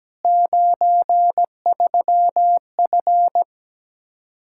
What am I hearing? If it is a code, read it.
Morse code, heard as a keyed tone, about 17 wpm, 700 Hz. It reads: 93F
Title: Practice QSO - Exchange 2 - Info - 17wpm